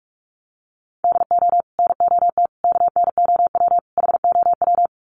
Morse code, heard as a keyed tone, about 44 wpm, 700 Hz. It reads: DO NOT KNOW HOW
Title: Sets of 4 Words 45wpm